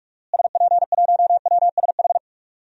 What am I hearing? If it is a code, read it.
Morse code, heard as a keyed tone, about 46 wpm, 700 Hz. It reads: SP1WSH
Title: DXCC-100 Call Signs - 6 Characters - Repeated 3x - 45wpm